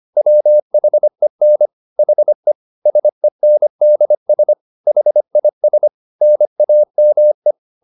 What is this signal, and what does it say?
Morse code, heard as a keyed tone, about 25 wpm, 600 Hz. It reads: WHEN HE SENDS HIS NAME